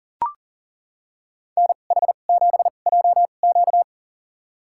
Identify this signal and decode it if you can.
Morse code, heard as a keyed tone, about 37 wpm, spaced as for 28 wpm, 700 Hz. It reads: NH7JQ
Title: USA Call Signs - 5 Characters - 2x Character Spacing - 40wpm